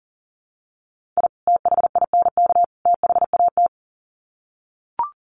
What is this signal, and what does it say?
Morse code, heard as a keyed tone, about 40 wpm, 700 Hz. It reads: I THINK THAT
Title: Sets of 3 Words 40wpm